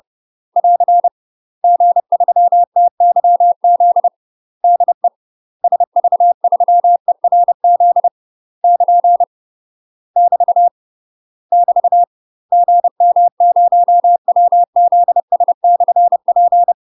Morse code, heard as a keyed tone, about 30 wpm, 700 Hz. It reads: <AR> G3TYZ DE SV3ERZ <KN> <BT> <BT> GM0WZS/P
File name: UK MASTER QSOs TRAINING / 30WPM / UK MASTER QSOs 0005